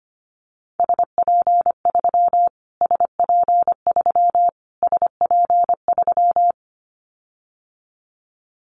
Morse code, heard as a keyed tone, about 25 wpm, 700 Hz. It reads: SP3 SP3 SP3
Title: Callsign Prefixes for Top 25 DXCC - Repeated 3x - 25wpm